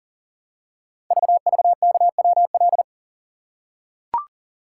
Morse code, heard as a keyed tone, about 40 wpm, 700 Hz. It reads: VVKWL
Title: Code Groups - 5 Letters - 40wpm